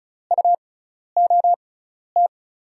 Morse code, heard as a keyed tone, about 36 wpm, spaced as for 12 wpm, 700 Hz. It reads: UOT
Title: USA Call Signs - 6 Characters - 6x Character Spacing - 35wpm